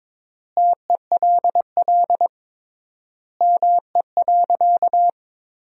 Morse code, heard as a keyed tone, about 22 wpm, 700 Hz. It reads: TELL ME.
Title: 3x Word-Spacing - Sentences from Top 200 Words - 22wpm